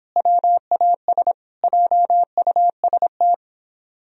Morse code, heard as a keyed tone, about 29 wpm, 700 Hz. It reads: WAS JUST